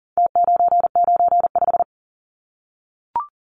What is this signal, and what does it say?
Morse code, heard as a keyed tone, about 40 wpm, 700 Hz. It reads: T995
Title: Numbers - Contest Style - 4-Digit 40wpm